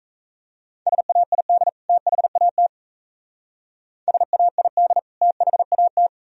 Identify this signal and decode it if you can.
Morse code, heard as a keyed tone, about 35 wpm, 700 Hz. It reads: SAID THAT SAID THAT
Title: Speed-Racing - Sets of 2 Words 28wpm